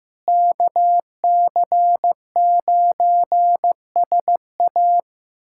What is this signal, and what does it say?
Morse code, heard as a keyed tone, about 15 wpm, 700 Hz. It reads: KC9SA
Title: USA Callsign DE Callsign - 5 Characters - 15wpm